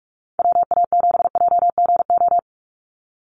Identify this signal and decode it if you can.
Morse code, heard as a keyed tone, about 44 wpm, 700 Hz. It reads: WA7JGO
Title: USA Call Signs 45wpm